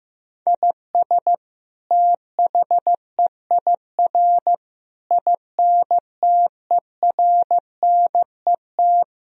Morse code, heard as a keyed tone, about 15 wpm, 700 Hz. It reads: IS THEIR INTERNET
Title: Sentences from Top 1000 Words - Review Part 4 - 15wpm